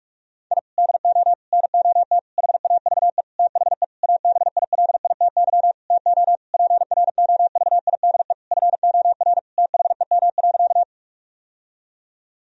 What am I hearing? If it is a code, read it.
Morse code, heard as a keyed tone, about 41 wpm, 700 Hz. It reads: I DO NOT HAVE THE ABILITY TO PROVIDE FOR THEM.
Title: Sentences from Top 1000 Words - Review Part 3 - 45wpm